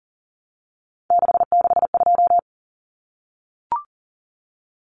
Morse code, heard as a keyed tone, about 40 wpm, 700 Hz. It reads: 662
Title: Numbers - 3-Digit 40wpm